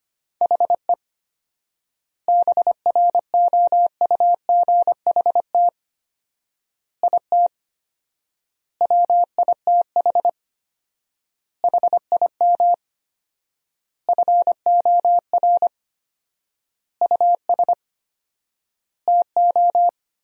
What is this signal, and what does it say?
Morse code, heard as a keyed tone, about 25 wpm, 700 Hz. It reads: HE BROUGHT IT WITH HIM FOR US TO
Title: 4x Word-Spacing - Sentences from Top 500 Words - 25wpm